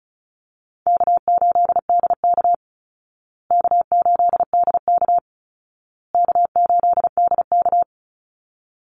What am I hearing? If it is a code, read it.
Morse code, heard as a keyed tone, about 34 wpm, 700 Hz. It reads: K8DK K8DK K8DK
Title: Call Signs Repeated 2x - 35wpm